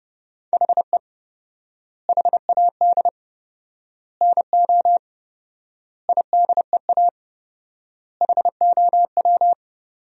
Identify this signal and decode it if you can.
Morse code, heard as a keyed tone, about 32 wpm, 700 Hz. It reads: HE HAD NO IDEA HOW
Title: 4x Word-Spacing - Sentences from Top 300 Words - 30wpm